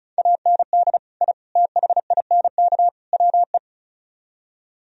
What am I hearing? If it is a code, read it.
Morse code, heard as a keyed tone, about 43 wpm, 700 Hz. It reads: AND I THINK WE
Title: Speed-Racing - Sets of 4 Words 35wpm